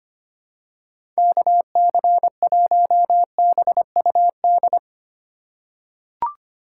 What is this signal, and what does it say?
Morse code, heard as a keyed tone, about 25 wpm, 700 Hz. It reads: KC1BUD
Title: USA Call Signs - 6 Characters - 3x Character Spacing - 25wpm